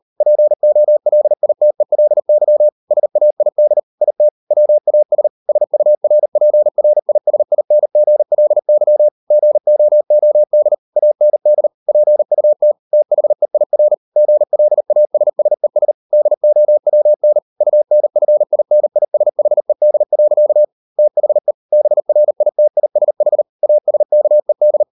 Morse code, heard as a keyed tone, about 40 wpm, 600 Hz. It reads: POLITELY SAID IT WAS SURPRISINGLY GOOD AND PUT THEIR GLASSES DOWN UNFINISHED. THE BRITISH ASKED